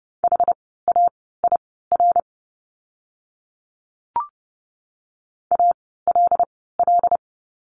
Morse code, heard as a keyed tone, about 30 wpm, spaced as for 17 wpm, 700 Hz. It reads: HAIR ALL
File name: Lesson 17f - Letter u - Words 30wpm